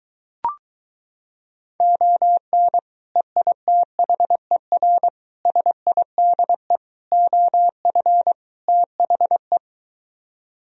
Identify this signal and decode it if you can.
Morse code, heard as a keyed tone, about 23 wpm, 700 Hz. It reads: ON EITHER SIDE OF THE